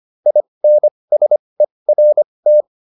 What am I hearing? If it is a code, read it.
Morse code, heard as a keyed tone, about 25 wpm, spaced as for 20 wpm, 600 Hz. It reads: INSERT
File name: SS315 20